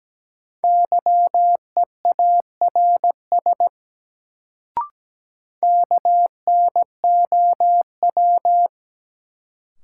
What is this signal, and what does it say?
Morse code, heard as a keyed tone, about 17 wpm, 700 Hz. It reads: YEARS KNOW
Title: Sets of 2 Words - New Word Drill - 17wpm